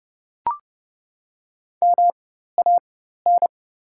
Morse code, heard as a keyed tone, about 30 wpm, spaced as for 14 wpm, 690 Hz. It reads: MAN
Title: 4x Character Spacing - Top 200 Words - 30wpm